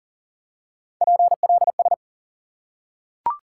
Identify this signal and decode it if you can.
Morse code, heard as a keyed tone, about 40 wpm, 700 Hz. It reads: PLS
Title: Top 90 QSO Elements 40wpm